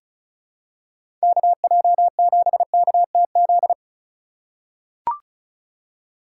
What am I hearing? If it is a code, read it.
Morse code, heard as a keyed tone, about 34 wpm, 700 Hz. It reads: KJ7KTZ
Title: USA Call Signs - 6 Characters - 8x Character Spacing - 35wpm